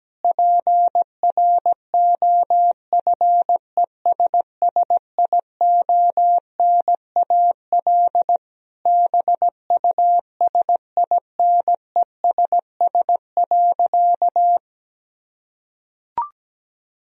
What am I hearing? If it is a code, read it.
Morse code, heard as a keyed tone, about 17 wpm, 700 Hz. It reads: PROFESSIONAL BUSINESS.
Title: Sentences from Top 1000 Words - Review Part 4 - 17wpm